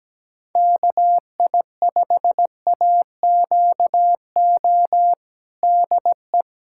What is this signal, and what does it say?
Morse code, heard as a keyed tone, about 17 wpm, 700 Hz. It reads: KI5AQO DE
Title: USA Callsign DE Callsign 17wpm